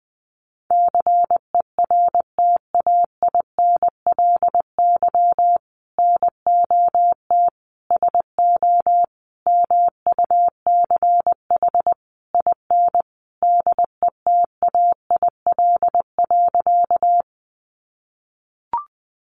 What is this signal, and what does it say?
Morse code, heard as a keyed tone, about 20 wpm, 700 Hz. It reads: CERTAINLY NOT SO MUCH IN DETAIL.
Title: Sentences from Top 1500 Words 20wpm